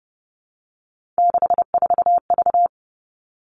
Morse code, heard as a keyed tone, about 30 wpm, 700 Hz. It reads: 64V
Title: All Cut Numbers - 3-Digit 30wpm